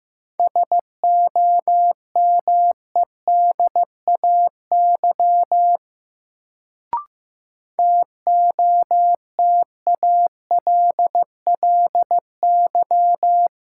Morse code, heard as a keyed tone, about 15 wpm, 700 Hz. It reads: SOMEDAY TOTALLY
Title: Sentences from Top 1600 Words - New Word Drill - 15wpm